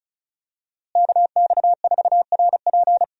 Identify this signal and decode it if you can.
Morse code, heard as a keyed tone, about 34 wpm, 700 Hz. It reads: KX4RP
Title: USA Call Signs - 5 Characters - Repeated 2x - 35wpm